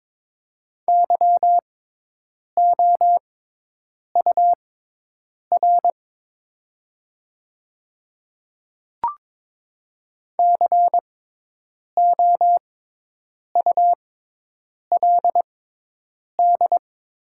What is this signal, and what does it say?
Morse code, heard as a keyed tone, about 22 wpm, spaced as for 8 wpm, 700 Hz. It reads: YOUR COULD
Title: 6x Character Spacing - Top 200 Words - 22wpm